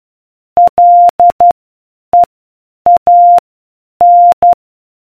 Morse code, heard as a keyed tone, about 12 wpm, spaced as for 8 wpm, 700 Hz. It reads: LEAN